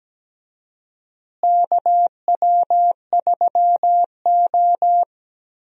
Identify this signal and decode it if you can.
Morse code, heard as a keyed tone, about 17 wpm, 700 Hz. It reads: KW3O
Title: USA Call Signs - 4 Characters - Repeated 3x - 17wpm